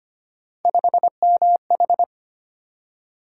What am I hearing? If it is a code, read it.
Morse code, heard as a keyed tone, about 25 wpm, 700 Hz. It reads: 5MH
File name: Three Letter-Number Combinations - 25wpm